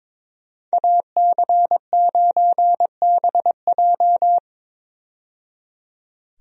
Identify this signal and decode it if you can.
Morse code, heard as a keyed tone, about 22 wpm, 700 Hz. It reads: AC9BJ
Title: USA Call Signs - 5 Characters - 22wpm